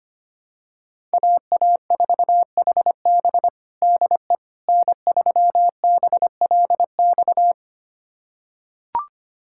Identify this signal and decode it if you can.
Morse code, heard as a keyed tone, about 25 wpm, 700 Hz. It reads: AA4HB DE N3BLX
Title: USA Callsign DE Callsign - 5 Characters - 25wpm